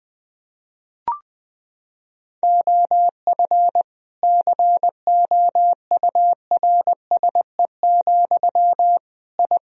Morse code, heard as a keyed tone, about 21 wpm, 700 Hz. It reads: OF COURSE, I
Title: Sentences from Top 1000 Words - Review Part 4 - 20wpm